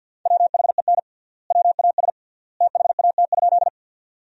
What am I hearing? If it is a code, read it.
Morse code, heard as a keyed tone, about 50 wpm, 700 Hz. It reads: WHEN WAS THAT?
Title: 3x Word-Spacing - Sentences from Top 100 Words - 50wpm